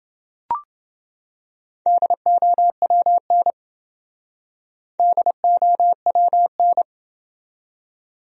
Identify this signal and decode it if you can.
Morse code, heard as a keyed tone, about 28 wpm, 700 Hz. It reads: DOWN DOWN